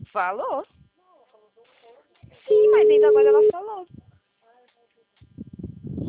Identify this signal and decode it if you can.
Ringback tone (Europe, 425 Hz), with speech.
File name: out- -2025- - - .15579